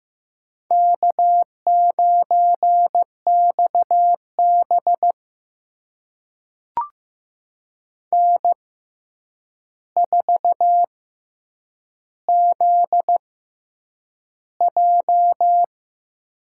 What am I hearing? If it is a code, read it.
Morse code, heard as a keyed tone, about 15 wpm, spaced as for 8 wpm, 700 Hz. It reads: K9XB N4ZJ
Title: USA Call Signs - 4 Characters - 6x Character Spacing - 15wpm